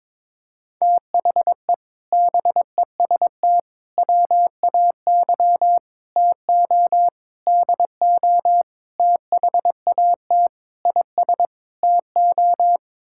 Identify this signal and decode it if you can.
Morse code, heard as a keyed tone, about 22 wpm, 700 Hz. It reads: THE BEST WAY TO DO THAT IS TO
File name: practice / 22wpm / Sets of 8 Words - 22wpm